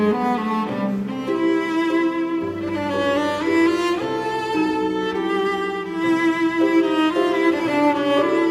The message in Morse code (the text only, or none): none